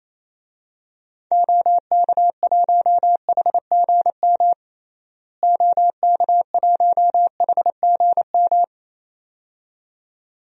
Morse code, heard as a keyed tone, about 27 wpm, 700 Hz. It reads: OK1HGM OK1HGM
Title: DXCC-100 Call Signs - 6 Characters - Repeated 2x - 28wpm